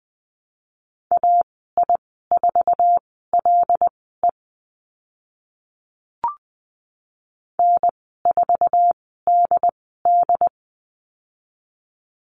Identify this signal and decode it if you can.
Morse code, heard as a keyed tone, about 20 wpm, spaced as for 14 wpm, 700 Hz. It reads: AI4LE N4DD
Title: 076 Lesson 13g - Letter L - Call Signs 20wpm